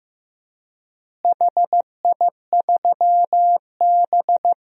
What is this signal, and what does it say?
Morse code, heard as a keyed tone, about 15 wpm, 700 Hz. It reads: HI3B
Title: DXCC-100 Call Signs - 4 Characters - Repeated 2x - 15wpm